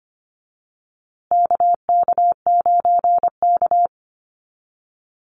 Morse code, heard as a keyed tone, about 25 wpm, 700 Hz. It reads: KK9K
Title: USA Call Signs Repeated 3x 25wpm